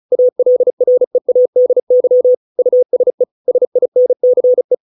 Morse code, heard as a keyed tone, about 34 wpm, 500 Hz. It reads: ALREADY USE SINCE